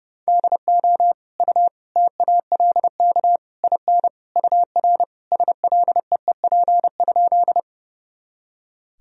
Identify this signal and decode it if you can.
Morse code, heard as a keyed tone, about 30 wpm, 700 Hz. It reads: DO U TALK IN UR SLEEP?
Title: Sentences from Top 500 Words and Abbreviations 30wpm